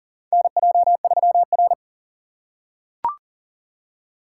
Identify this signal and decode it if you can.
Morse code, heard as a keyed tone, about 40 wpm, 700 Hz. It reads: NJ3R
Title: USA Call Signs - 4 Characters - 4x Character Spacing - 40wpm